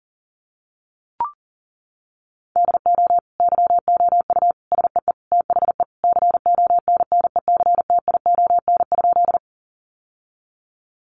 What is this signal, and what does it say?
Morse code, heard as a keyed tone, about 40 wpm, 700 Hz. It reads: DO YOU SEE THE CONNECTION?
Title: Sentences from Top 1300 Words 40wpm